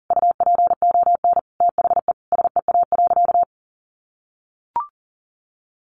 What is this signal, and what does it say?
Morse code, heard as a keyed tone, about 40 wpm, 700 Hz. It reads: UPON THE SEA.
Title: Sentences from Top 1500 Words 40wpm